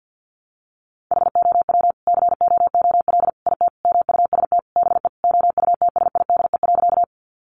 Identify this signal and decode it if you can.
Morse code, heard as a keyed tone, about 50 wpm, 700 Hz. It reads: HOW COOL IT MUST BE OUTSIDE.
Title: Sentences from Top 1200 Words 50wpm